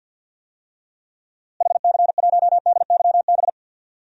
Morse code, heard as a keyed tone, about 49 wpm, 700 Hz. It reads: HC1DYB
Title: DXCC-100 Call Signs - 6 Characters - Repeated 2x - 50wpm